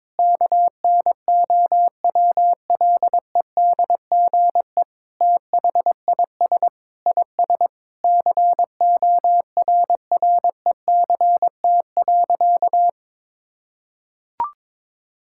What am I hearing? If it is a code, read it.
Morse code, heard as a keyed tone, about 22 wpm, 700 Hz. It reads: KNOWLEDGE THIS IS CORRECT.
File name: practice / 22wpm / Sentences with 8-Word Phrases - 22wpm